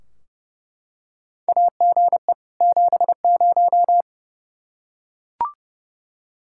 Morse code, heard as a keyed tone, about 30 wpm, 700 Hz. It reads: AGE 70
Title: Practice QSO - Exchange 3 - More Info - 30wpm